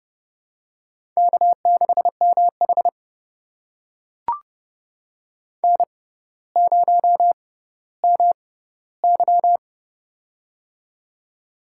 Morse code, heard as a keyed tone, about 32 wpm, spaced as for 16 wpm, 700 Hz. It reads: K6MH N0MY
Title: USA Call Signs - 4 Characters - 6x Character Spacing - 30wpm